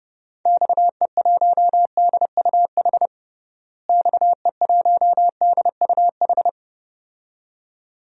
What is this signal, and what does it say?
Morse code, heard as a keyed tone, about 30 wpm, 700 Hz. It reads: XE1DUH XE1DUH
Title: DXCC-100 Call Signs - 6 Characters - Repeated 2x - 30wpm